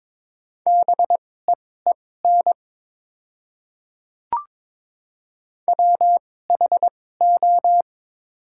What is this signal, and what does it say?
Morse code, heard as a keyed tone, about 21 wpm, spaced as for 16 wpm, 700 Hz. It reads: BEEN WHO
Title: 2x Character Spacing - Top 500 Words - 22wpm